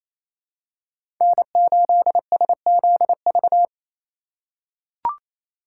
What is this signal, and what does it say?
Morse code, heard as a keyed tone, about 28 wpm, 700 Hz. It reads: N8SZV
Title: USA Call Signs - 5 Characters - 4x Character Spacing - 28wpm